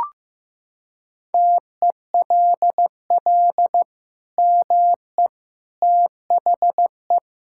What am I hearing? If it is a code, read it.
Morse code, heard as a keyed tone, about 16 wpm, 700 Hz. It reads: TELL ME THE
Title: Sentences from Top 1000 Words - Review Part 5 - 15wpm